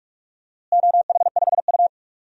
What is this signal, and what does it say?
Morse code, heard as a keyed tone, about 44 wpm, 700 Hz. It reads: OH5U
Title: DXCC-100 Call Signs - 4 Characters - Repeated 3x - 45wpm